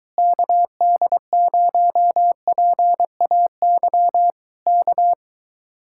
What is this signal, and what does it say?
Morse code, heard as a keyed tone, about 23 wpm, 700 Hz. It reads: KD0PAY K